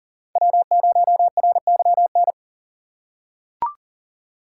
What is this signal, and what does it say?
Morse code, heard as a keyed tone, about 40 wpm, 700 Hz. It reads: W0RYN
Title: USA Call Signs - 5 Characters - 4x Character Spacing - 40wpm